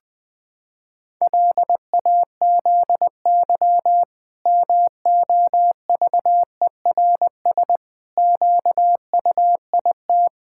Morse code, heard as a keyed tone, about 20 wpm, 700 Hz. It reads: LAZY MOVERS QUIT
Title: Pangram Challenge v2 - 20wpm